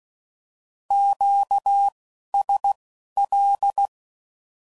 Morse code, heard as a keyed tone, about 16 wpm, spaced as for 11 wpm, 800 Hz. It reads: QSL
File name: CW Code Groups @10WPM / Q Calls / Q Calls Once